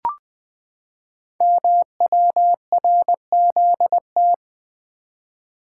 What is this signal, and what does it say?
Morse code, heard as a keyed tone, about 21 wpm, 700 Hz. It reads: MWRZT
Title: Code Groups - 5 Letters - 20wpm